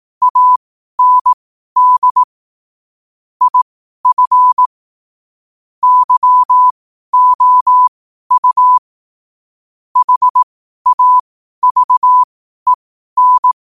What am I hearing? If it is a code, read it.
Morse code, heard as a keyed tone, about 18 wpm, spaced as for 12 wpm, 1000 Hz. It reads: AND IF YOU HAVEN
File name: qotdfiles / 12wpm